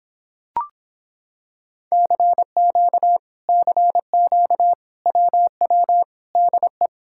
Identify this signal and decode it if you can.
Morse code, heard as a keyed tone, about 26 wpm, 700 Hz. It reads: CQ CQ WW DE